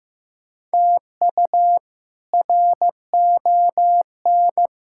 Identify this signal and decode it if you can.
Morse code, heard as a keyed tone, about 15 wpm, 700 Hz. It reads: TU RON
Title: K1USN SST QSO Practice - 15wpm